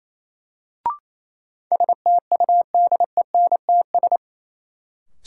Morse code, heard as a keyed tone, about 28 wpm, 700 Hz. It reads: STUDENTS
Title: Sets of 5 Words Encore - New Word Drill - 28wpm